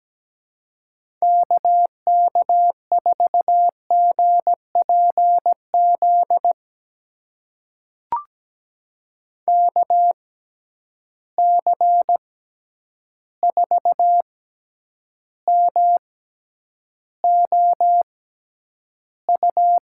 Morse code, heard as a keyed tone, about 17 wpm, spaced as for 9 wpm, 700 Hz. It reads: KK4GPZ KC4MOU
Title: USA Call Signs - 6 Characters - 6x Character Spacing - 17wpm